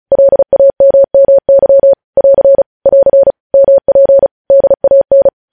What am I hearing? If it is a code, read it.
Morse code, heard as a keyed tone, about 36 wpm, 550 Hz. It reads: LAMMY <AR> <AR> MP DAN
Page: News Headlines in Morse Code at 35 WPM